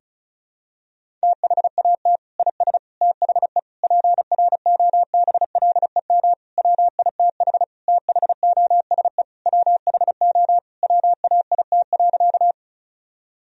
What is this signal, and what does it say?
Morse code, heard as a keyed tone, about 36 wpm, 700 Hz. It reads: THAT IS THE PROBLEM WITH THOSE WHO WAIT.
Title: Sentences from Top 1000 Words - Review Part 2 - 35wpm